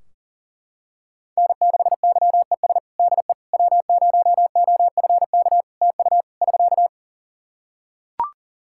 Morse code, heard as a keyed tone, about 40 wpm, 700 Hz. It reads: N6YES DE W0OFK TU <SK>
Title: K1USN SST QSO Practice 40wpm